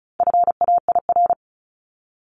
Morse code, heard as a keyed tone, about 34 wpm, 700 Hz. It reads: FAIR
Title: Top 1251-1500 Words 35wpm